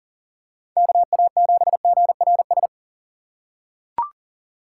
Morse code, heard as a keyed tone, about 40 wpm, 700 Hz. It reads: KA7GRS
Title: USA Call Signs - 6 Characters - 4x Character Spacing - 40wpm